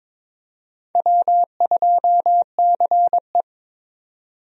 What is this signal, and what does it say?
Morse code, heard as a keyed tone, about 22 wpm, 700 Hz. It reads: W2CE
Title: USA Call Signs - 4 Characters - 22wpm